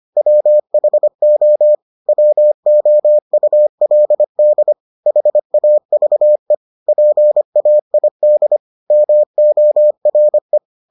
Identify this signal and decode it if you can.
Morse code, heard as a keyed tone, about 25 wpm, 600 Hz. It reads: WHO WOULD HAVE PAID MORE